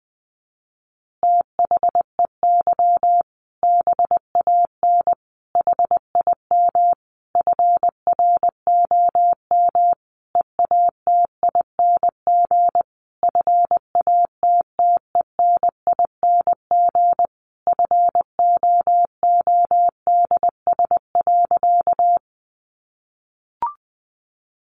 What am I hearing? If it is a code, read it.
Morse code, heard as a keyed tone, about 20 wpm, 700 Hz. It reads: THEY BAN HIM FROM EATING FATTENING FOODS.
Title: Easy Phrases 20wpm